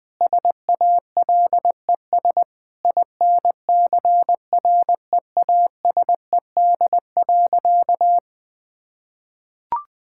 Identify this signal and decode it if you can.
Morse code, heard as a keyed tone, about 21 wpm, 700 Hz. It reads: SALES INCREASED.
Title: Sentences with 8-Word Phrases - 20wpm